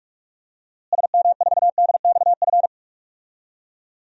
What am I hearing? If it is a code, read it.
Morse code, heard as a keyed tone, about 41 wpm, 700 Hz. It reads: SM4DXF
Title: DXCC-100 Call Signs - 6 Characters - Repeated 2x - 45wpm